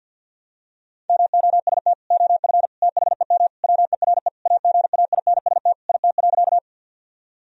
Morse code, heard as a keyed tone, about 51 wpm, 700 Hz. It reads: MOST OF THEM WERE AGAINST IT.
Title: Sentences from Top 1000 Words - Review Part 1 - 50wpm